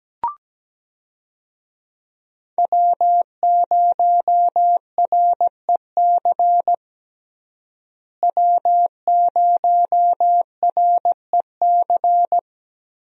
Morse code, heard as a keyed tone, about 17 wpm, 700 Hz. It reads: W0REC W0REC
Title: USA Call Signs - 5 Characters - Repeated 3x - 17wpm